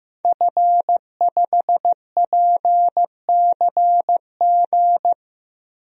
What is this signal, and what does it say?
Morse code, heard as a keyed tone, about 15 wpm, 700 Hz. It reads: F5PCG
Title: DXCC-100 Call Signs - 5 Characters - 15wpm